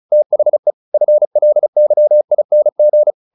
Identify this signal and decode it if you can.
Morse code, heard as a keyed tone, about 35 wpm, 600 Hz. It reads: THE FLYING
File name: the old man and the sea / the old man and the sea 1 35wpm